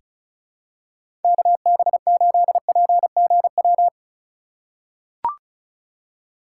Morse code, heard as a keyed tone, about 36 wpm, 700 Hz. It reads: KB8PGW
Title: K1USN SST QSO Practice - 35wpm